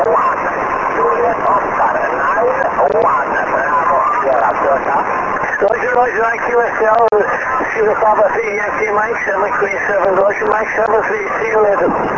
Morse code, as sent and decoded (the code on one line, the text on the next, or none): none